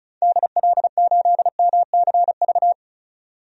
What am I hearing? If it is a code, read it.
Morse code, heard as a keyed tone, about 34 wpm, 700 Hz. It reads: DL8MCV
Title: DXCC-100 Call Signs - 6 Characters - 35wpm